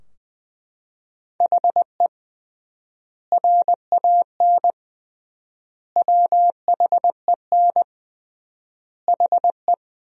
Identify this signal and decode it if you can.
Morse code, heard as a keyed tone, about 20 wpm, 700 Hz. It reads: HE RAN WHEN HE
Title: 3x Word-Spacing - Sentences from Top 500 Words - 20wpm